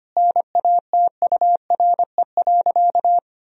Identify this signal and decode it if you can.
Morse code, heard as a keyed tone, about 25 wpm, 700 Hz. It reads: NATURE.
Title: Sentences from Top 1000 Words - Review Part 4 - 25wpm